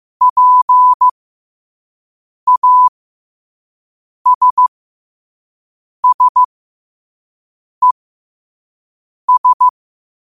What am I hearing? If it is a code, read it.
Morse code, heard as a keyed tone, about 15 wpm, spaced as for 5 wpm, 1000 Hz. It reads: PASSES
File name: qotdfiles / 5wpm